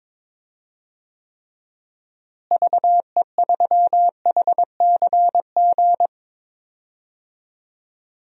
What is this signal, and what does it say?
Morse code, heard as a keyed tone, about 21 wpm, 700 Hz. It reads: VE3HCG
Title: DXCC-100 Call Signs - 6 Characters - Repeated 3x - 22wpm